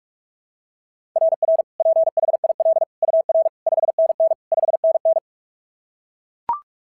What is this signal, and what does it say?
Morse code, heard as a keyed tone, about 42 wpm, 690 Hz. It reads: RR PHIL UR 5NN 5NN
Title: SOTA QSO Practice 45wpm